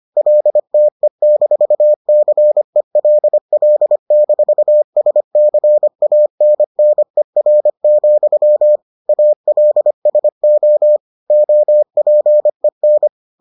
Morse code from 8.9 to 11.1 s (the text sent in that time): ALSO